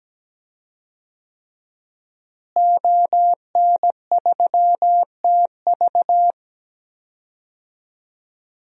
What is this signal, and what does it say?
Morse code, heard as a keyed tone, about 17 wpm, 700 Hz. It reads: ON3TV
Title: DXCC-100 Call Signs - 5 Characters - Repeated 3x - 17wpm